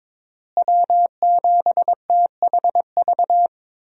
Morse code, heard as a keyed tone, about 22 wpm, 700 Hz. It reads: W7THV
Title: USA Call Signs - 5 Characters - Repeated 2x - 22wpm